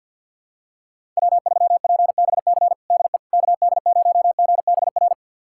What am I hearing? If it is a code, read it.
Morse code, heard as a keyed tone, about 51 wpm, 700 Hz. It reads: W3PBC DE KD0GBR
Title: USA Callsign DE Callsign 50wpm